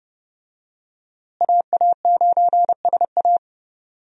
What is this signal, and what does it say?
Morse code, heard as a keyed tone, about 32 wpm, 700 Hz. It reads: AA9SA
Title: USA Call Signs - 5 Characters - Repeated 2x - 30wpm